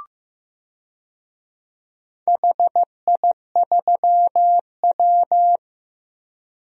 Morse code, heard as a keyed tone, about 15 wpm, 700 Hz. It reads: HI3W
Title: DXCC-100 Call Signs - 4 Characters - Repeated 2x - 15wpm